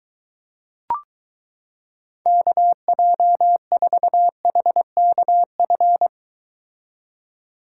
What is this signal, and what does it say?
Morse code, heard as a keyed tone, about 23 wpm, 700 Hz. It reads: KJ4HKF